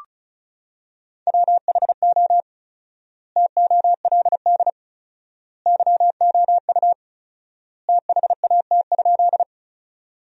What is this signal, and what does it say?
Morse code, heard as a keyed tone, about 34 wpm, 700 Hz. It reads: WHO TOLD YOU THAT?
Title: 4x Word-Spacing - Sentences from Top 400 Words - 35wpm